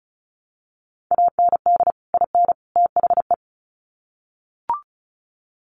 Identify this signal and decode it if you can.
Morse code, heard as a keyed tone, about 34 wpm, 700 Hz. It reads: AND IN THE
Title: Sets of 3 Words 35wpm